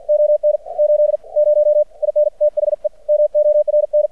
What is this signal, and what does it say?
Morse code, heard as a keyed tone, about 50 wpm, spaced as for 44 wpm, 600 Hz. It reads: ON 80 IN THE MORN
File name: archived / CRSnet